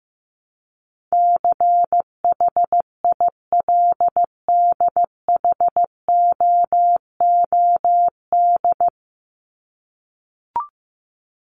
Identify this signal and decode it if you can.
Morse code, heard as a keyed tone, about 14 wpm, 700 Hz. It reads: CHILDHOOD
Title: Top 1751-2000 Words 15wpm